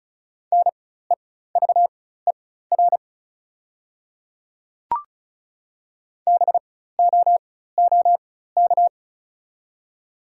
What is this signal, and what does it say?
Morse code, heard as a keyed tone, about 36 wpm, spaced as for 16 wpm, 690 Hz. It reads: NEVER BOOK
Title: 4x Character Spacing - Top 500 Words - 35wpm